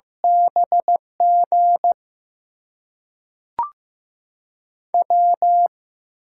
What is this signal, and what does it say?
Morse code, heard as a keyed tone, about 15 wpm, spaced as for 5 wpm, 700 Hz. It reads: BGW
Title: USA Call Signs - 6 Characters - 6x Character Spacing - 15wpm